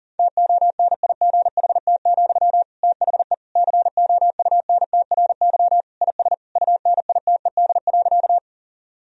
Morse code, heard as a keyed tone, about 40 wpm, 700 Hz. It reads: TONIGHT, THE COUNTRY IS UNITED.
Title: Sentences from Top 1000 Words - Review Part 4 - 40wpm